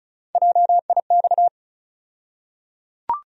Morse code, heard as a keyed tone, about 36 wpm, 700 Hz. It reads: JIX
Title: USA Call Sign Postfixes - 3 Letters 35wpm